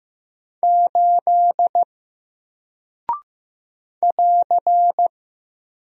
Morse code, heard as a keyed tone, about 15 wpm, 700 Hz. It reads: 8 <AR>
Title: Single Character (Letter:Number:Punctuation:Prosign) Alt 15wpm